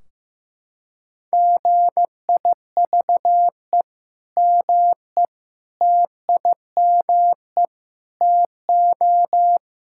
Morse code, heard as a keyed tone, about 15 wpm, 700 Hz. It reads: GIVE ME TIME TO
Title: Sentences from Top 1600 Words 15wpm